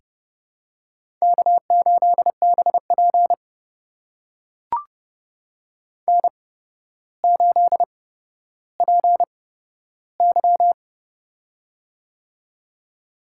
Morse code, heard as a keyed tone, about 32 wpm, spaced as for 13 wpm, 700 Hz. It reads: K8BP N8PY
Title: USA Call Signs - 4 Characters - 8x Character Spacing - 30wpm